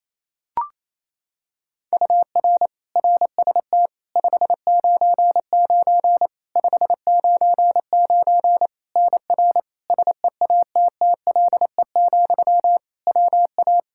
Morse code, heard as a keyed tone, about 28 wpm, 700 Hz. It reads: UR RST 599 599 NR SEATTLE, WA
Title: Practice QSO - Exchange 1 - Beginning - 28wpm